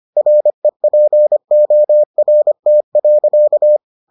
Morse code, heard as a keyed tone, about 25 wpm, 600 Hz. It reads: REPORT.